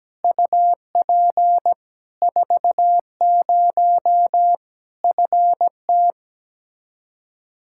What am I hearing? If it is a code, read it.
Morse code, heard as a keyed tone, about 17 wpm, 700 Hz. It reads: UP 40 FT
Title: Practice QSO - Exchange 2 - Info - 17wpm